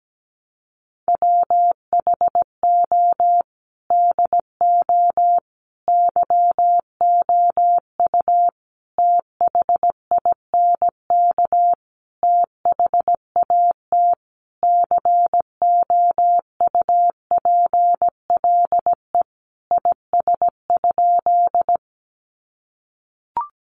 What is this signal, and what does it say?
Morse code, heard as a keyed tone, about 17 wpm, 700 Hz. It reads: WHO DO YOU THINK THAT COUPLE IS?
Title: Sentences from Top 1100 Words 17wpm